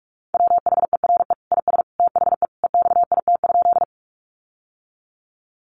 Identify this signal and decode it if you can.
Morse code, heard as a keyed tone, about 44 wpm, 700 Hz. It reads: WHERE IS THE EXIT?
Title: Sentences from Top 1100 Words 45wpm